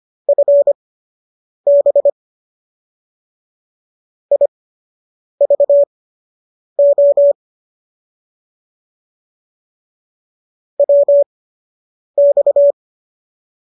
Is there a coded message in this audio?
Yes: Morse code from 0.3 to 12.7 s, heard as a keyed tone, about 25 wpm, spaced as for 8 wpm, 570 Hz. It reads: FB IVO WX